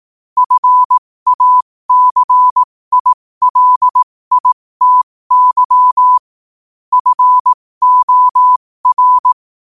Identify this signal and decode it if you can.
Morse code, heard as a keyed tone, about 18 wpm, spaced as for 15 wpm, 1000 Hz. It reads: FACILITY FOR
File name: 15wpm